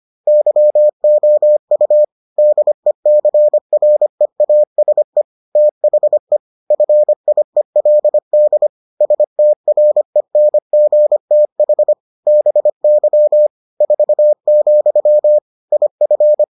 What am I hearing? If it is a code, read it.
Morse code, heard as a keyed tone, about 25 wpm, 600 Hz. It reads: YOU DECREASE THE FIELD STRENGTH BY 4, IF